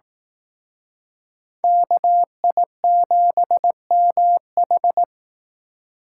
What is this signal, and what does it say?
Morse code, heard as a keyed tone, about 18 wpm, 700 Hz. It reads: KI7MH
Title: Speed Racing - USA 2x2 Call Signs 15wpm